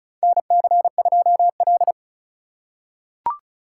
Morse code, heard as a keyed tone, about 36 wpm, 700 Hz. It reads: NC2L
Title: USA Call Signs - 4 Characters - 2x Character Spacing - 35wpm